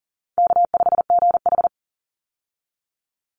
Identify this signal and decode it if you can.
Morse code, heard as a keyed tone, about 40 wpm, 700 Hz. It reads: K5GH
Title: Call Signs Repeated 2x - 40wpm